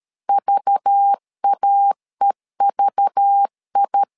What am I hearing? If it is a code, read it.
Morse code, heard as a keyed tone, about 13 wpm, 800 Hz. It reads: VAEVI
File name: LEZIONI / 5 LBVC / 5 LBV 12wpm 1855